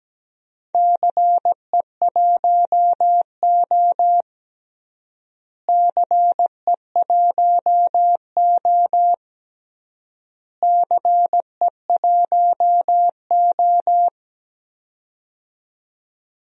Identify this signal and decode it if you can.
Morse code, heard as a keyed tone, about 17 wpm, 700 Hz. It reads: CE1O CE1O CE1O
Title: DXCC-100 Call Signs - 4 Characters - Repeated 3x - 17wpm